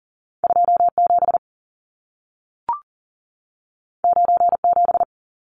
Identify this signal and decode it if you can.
Morse code, heard as a keyed tone, about 40 wpm, 700 Hz. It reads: 27 97
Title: Numbers - 2-Digit 40wpm